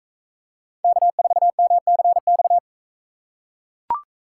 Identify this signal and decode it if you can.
Morse code, heard as a keyed tone, about 42 wpm, 700 Hz. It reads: K4MCX